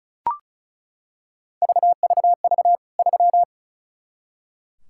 Morse code, heard as a keyed tone, about 36 wpm, 700 Hz. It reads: VVV 3
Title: K1USN SST QSO Practice 35wpm